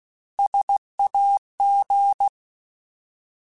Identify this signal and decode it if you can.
Morse code, heard as a keyed tone, about 16 wpm, 800 Hz. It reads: SAG
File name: Alphabetical / Alphabetical mp3s / Ss @15